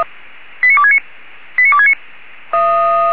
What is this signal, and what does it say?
Five-tone selective call (ZVEI), heard as quick runs of five tones, tones of about 70 ms. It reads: 89269 89269